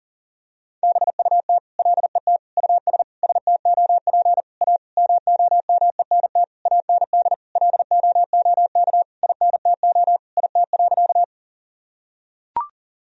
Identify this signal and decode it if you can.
Morse code, heard as a keyed tone, about 40 wpm, 700 Hz. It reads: BUT LET US STOP A MOMENT AND LOOK INTO IT.
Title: Sentences from Top 1000 Words - Review Part 1 - 40wpm